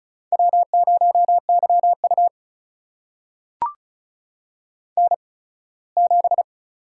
Morse code, heard as a keyed tone, about 36 wpm, spaced as for 21 wpm, 700 Hz. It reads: W0YU N7
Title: USA Call Signs - 8x Character Spacing - 35wpm